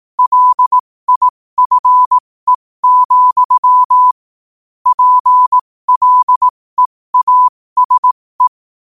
Morse code, heard as a keyed tone, about 18 wpm, spaced as for 15 wpm, 1000 Hz. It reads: LIFE, PLEASE
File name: qotdfiles / 15wpm